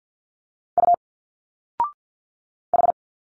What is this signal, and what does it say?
Morse code, heard as a keyed tone, about 51 wpm, 700 Hz. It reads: U H
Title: Single Letters 50wpm